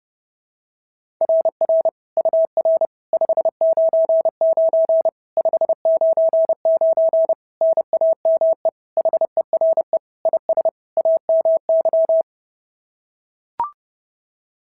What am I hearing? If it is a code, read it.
Morse code, heard as a keyed tone, about 30 wpm, 650 Hz. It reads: RR UR 599 599 NAME HERE IS AMY
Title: SOTA QSO Practice 30wpm